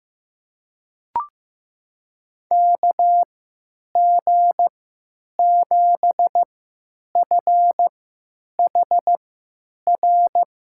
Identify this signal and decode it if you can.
Morse code, heard as a keyed tone, about 15 wpm, spaced as for 9 wpm, 700 Hz. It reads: KG7FHR
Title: USA Call Signs - 3x Character Spacing - 15wpm